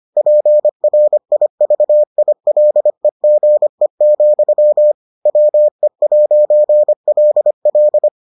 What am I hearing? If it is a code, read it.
Morse code, heard as a keyed tone, about 25 wpm, 600 Hz. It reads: PRIVILEGE, WE'LL